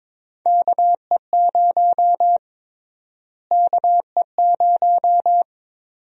Morse code, heard as a keyed tone, about 22 wpm, 700 Hz. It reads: KE0 KE0
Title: USA Call Sign Prefixes - Repeated 2x - 22wpm